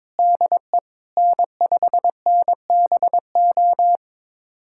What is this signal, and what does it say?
Morse code, heard as a keyed tone, about 21 wpm, 700 Hz. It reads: DE N5NBO
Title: USA Callsign DE Callsign 22wpm